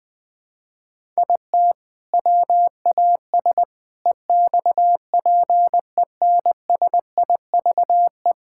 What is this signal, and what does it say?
Morse code, heard as a keyed tone, about 20 wpm, 700 Hz. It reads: IT WAS EXPENSIVE
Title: Sentences from Top 1000 Words - Review Part 6 - 20wpm